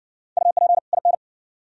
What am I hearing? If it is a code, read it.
Morse code, heard as a keyed tone, about 57 wpm, 700 Hz. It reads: UP IN